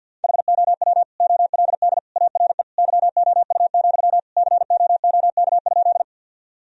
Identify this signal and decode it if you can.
Morse code, heard as a keyed tone, about 50 wpm, 700 Hz. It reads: HOW OLD ARE YOU, COOK?
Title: Sentences from Top 1000 Words - Review Part 4 - 50wpm